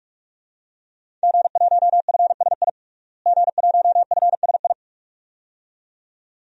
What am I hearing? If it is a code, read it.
Morse code, heard as a keyed tone, about 46 wpm, 700 Hz. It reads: G1FSI G1FSI
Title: DXCC-100 Call Signs - 5 Characters - Repeated 2x - 45wpm